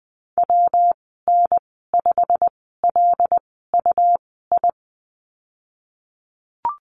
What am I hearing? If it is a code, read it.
Morse code, heard as a keyed tone, about 20 wpm, spaced as for 15 wpm, 700 Hz. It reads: WN5LUI
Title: 128 Lesson 21g - Number 6 - Call Signs 20wpm